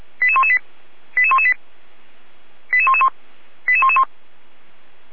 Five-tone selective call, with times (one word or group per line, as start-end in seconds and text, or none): 0.2-0.6 88118
1.2-1.5 88118
2.7-3.1 88111
3.7-4.0 88111